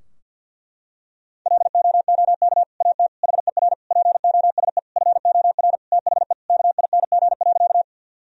Morse code, heard as a keyed tone, about 50 wpm, 700 Hz. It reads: LOOK AT HER POSE FOR THE KING.
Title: Sentences from Top 1000 Words - Review Part 2 - 50wpm